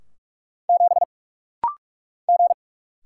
Morse code, heard as a keyed tone, about 44 wpm, 700 Hz. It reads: 7 G
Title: Single Letter-Number - Rapid-Fire 45wpm